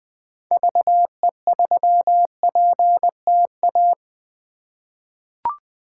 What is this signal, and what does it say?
Morse code, heard as a keyed tone, about 20 wpm, 700 Hz. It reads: VE3PTA
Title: 2023 POTA Hunter Call Signs - 20wpm